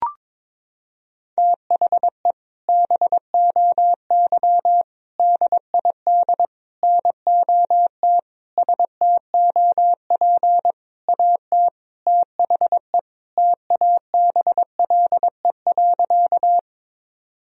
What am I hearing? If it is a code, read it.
Morse code, heard as a keyed tone, about 22 wpm, 700 Hz. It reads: THE BOY DID NOT STOP AT THE TABLE.
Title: Sentences from Top 1000 Words - Review Part 2 - 22wpm